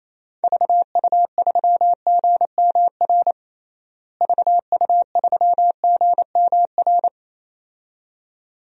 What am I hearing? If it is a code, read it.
Morse code, heard as a keyed tone, about 28 wpm, 700 Hz. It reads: VU3GMR VU3GMR
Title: DXCC-100 Call Signs - 6 Characters - Repeated 2x - 28wpm